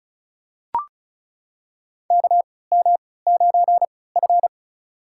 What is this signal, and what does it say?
Morse code, heard as a keyed tone, about 36 wpm, spaced as for 20 wpm, 700 Hz. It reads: KM9F
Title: USA Call Signs - 4 Characters - 3x Character Spacing - 35wpm